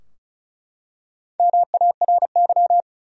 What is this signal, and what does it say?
Morse code, heard as a keyed tone, about 36 wpm, 700 Hz. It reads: MARY
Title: USA Common Names - Max 4 Letters - 35wpm